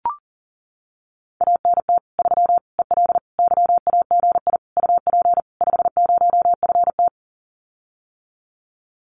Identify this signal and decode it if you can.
Morse code, heard as a keyed tone, about 40 wpm, 700 Hz. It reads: ANT 3 EL YAGI UP 50FT
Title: Common QSO Phrases Encore 40wpm